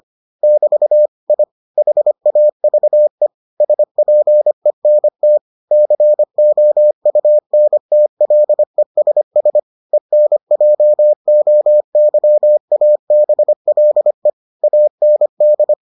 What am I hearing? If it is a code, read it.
Morse code, heard as a keyed tone, about 25 wpm, 600 Hz. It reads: <BT> I HAVE SPENT COUNTLESS ENJOYABLE AND